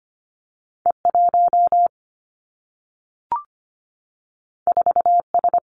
Morse code, heard as a keyed tone, about 25 wpm, 700 Hz. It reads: E 1 4 S
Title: Two Challenging Letter-Number Combinations 25wpm